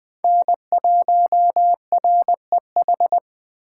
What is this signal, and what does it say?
Morse code, heard as a keyed tone, about 20 wpm, 700 Hz. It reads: N1REH
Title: Speed Racing - USA 1x3 Call Signs 17wpm